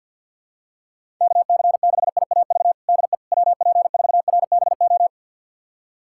Morse code, heard as a keyed tone, about 47 wpm, 700 Hz. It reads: KC6IAV DE WP4RBO
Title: USA Callsign DE Callsign 50wpm